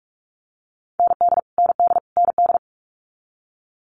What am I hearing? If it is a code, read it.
Morse code, heard as a keyed tone, about 46 wpm, 700 Hz. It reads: ND ND ND
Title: Multi-letter Phonemes Repeated 3x 45wpm